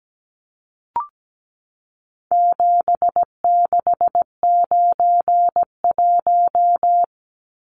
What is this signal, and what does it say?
Morse code, heard as a keyed tone, about 17 wpm, 700 Hz. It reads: 7691
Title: Numbers - 4-Digit 17wpm